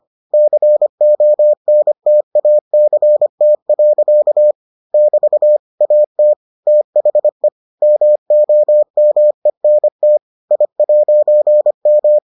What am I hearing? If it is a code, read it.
Morse code, heard as a keyed tone, about 25 wpm, 600 Hz. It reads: CONTACT. <BT> AT THE MOMENT I'M